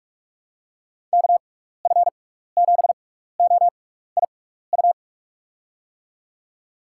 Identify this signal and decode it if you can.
Morse code, heard as a keyed tone, about 44 wpm, spaced as for 15 wpm, 700 Hz. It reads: KF7OIU
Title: USA Call Signs - 6 Characters - 6x Character Spacing - 45wpm